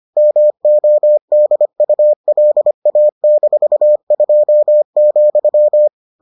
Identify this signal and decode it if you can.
Morse code, heard as a keyed tone, about 25 wpm, 600 Hz. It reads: MODULA-2,